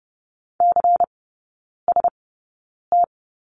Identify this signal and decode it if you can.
Morse code, heard as a keyed tone, about 30 wpm, spaced as for 9 wpm, 700 Hz. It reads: CST
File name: Lesson 18e - Letter m - Three Characters 30wpm